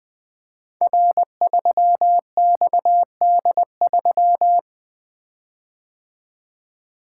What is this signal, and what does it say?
Morse code, heard as a keyed tone, about 20 wpm, 700 Hz. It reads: R3XD3
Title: Code Groups - 5 Letter:Numbers - 20wpm